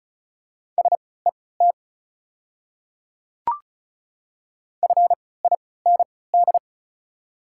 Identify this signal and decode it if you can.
Morse code, heard as a keyed tone, about 34 wpm, spaced as for 20 wpm, 690 Hz. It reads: SET FIND
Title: 3x Character Spacing - Top 200 Words - 35wpm